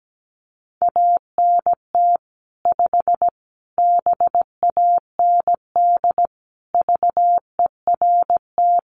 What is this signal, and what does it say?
Morse code, heard as a keyed tone, about 17 wpm, 700 Hz. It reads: ANT 5 BAND VERT
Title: Common QSO Phrases 17wpm